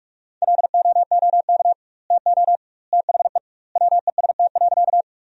Morse code, heard as a keyed tone, about 46 wpm, 700 Hz. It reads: LOOK TO THE WEST.
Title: 2x Word-Spacing - Sentences from Top 400 Words - 45wpm